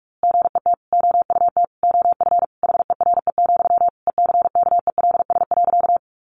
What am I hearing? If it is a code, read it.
Morse code, heard as a keyed tone, about 44 wpm, 700 Hz. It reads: GET OUT OF HERE, ECKELS.
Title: A Sound Of Thunder 45wpm